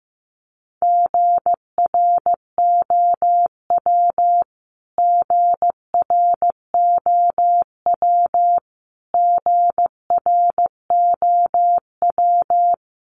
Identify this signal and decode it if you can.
Morse code, heard as a keyed tone, about 15 wpm, 700 Hz. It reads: GROW GROW GROW
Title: Top 300 Words Repeated 15wpm